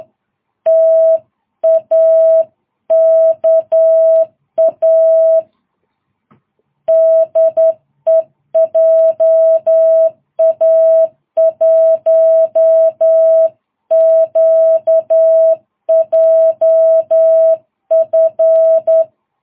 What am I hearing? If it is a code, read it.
Morse code, heard as a keyed tone, about 10 wpm, 660 Hz. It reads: TAKA DE JA1QJF
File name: jh8cbh / 2022cw